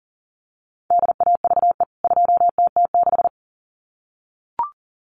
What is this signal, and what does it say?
Morse code, heard as a keyed tone, about 40 wpm, 700 Hz. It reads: DAVE 2TT6
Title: CWT Exchange 40wpm